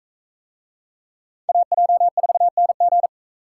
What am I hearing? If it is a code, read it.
Morse code, heard as a keyed tone, about 44 wpm, 700 Hz. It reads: AJ4NG